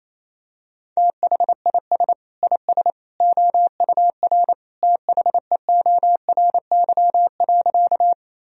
Morse code, heard as a keyed tone, about 28 wpm, 700 Hz. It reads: THIS IS OUR THEORY.
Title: Sentences from Top 1000 Words - Review Part 3 - 28wpm